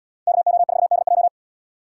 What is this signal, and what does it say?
Morse code, heard as a keyed tone, about 77 wpm, 700 Hz. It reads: KO4DJ